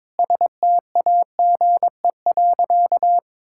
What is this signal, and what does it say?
Morse code, heard as a keyed tone, about 22 wpm, 700 Hz. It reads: STAGE.
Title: Sentences from Top 1600 Words 22wpm